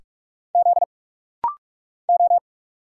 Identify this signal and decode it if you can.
Morse code, heard as a keyed tone, about 44 wpm, 700 Hz. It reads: Z O
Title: Single Letter-Number Alt - Rapid-Fire 45wpm